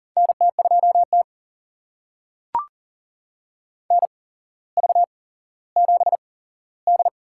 Morse code, heard as a keyed tone, about 40 wpm, spaced as for 17 wpm, 700 Hz. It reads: NT2T NV7D
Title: USA Call Signs - 4 Characters - 8x Character Spacing - 40wpm